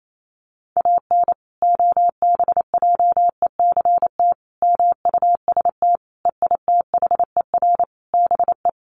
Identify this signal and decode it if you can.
Morse code, heard as a keyed tone, about 28 wpm, 700 Hz. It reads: AN OBJECT MUST EITHER BE
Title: Sentences from Top 1400 Words 28wpm